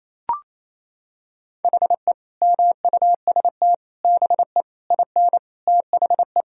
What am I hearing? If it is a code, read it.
Morse code, heard as a keyed tone, about 28 wpm, 700 Hz. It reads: HE MUST BE IN THE
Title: Sentences from Top 1000 Words - Review Part 2 - 28wpm